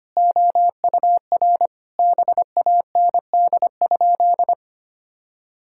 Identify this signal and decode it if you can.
Morse code, heard as a keyed tone, about 25 wpm, 700 Hz. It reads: OUR BAND?
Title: Sentences from Top 1600 Words 25wpm